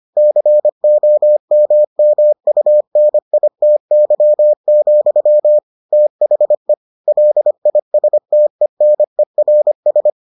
Morse code, heard as a keyed tone, about 25 wpm, 600 Hz. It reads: COMMUNITY, THE LISTENERS